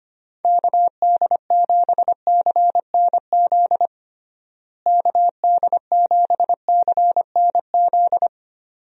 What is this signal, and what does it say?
Morse code, heard as a keyed tone, about 25 wpm, 700 Hz. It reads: KD7CNZ KD7CNZ
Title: USA Call Signs - 6 Characters - Repeated 2x - 25wpm